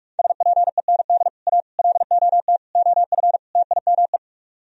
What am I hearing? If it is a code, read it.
Morse code, heard as a keyed tone, about 50 wpm, 700 Hz. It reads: SPEND A LOT OF TIME